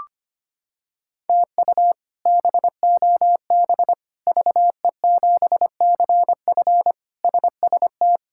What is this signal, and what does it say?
Morse code, heard as a keyed tone, about 25 wpm, 700 Hz. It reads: TU BOB VE7CF SST
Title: K1USN SST QSO Practice - 25wpm